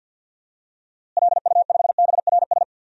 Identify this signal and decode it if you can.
Morse code, heard as a keyed tone, about 52 wpm, 700 Hz. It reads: LU5BRS
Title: DXCC-100 Call Signs - 6 Characters - Repeated 3x - 50wpm